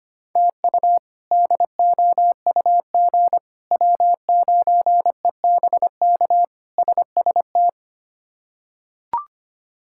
Morse code, heard as a keyed tone, about 25 wpm, 700 Hz. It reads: TU DOUG W9EBK SST
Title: K1USN SST QSO Practice - 25wpm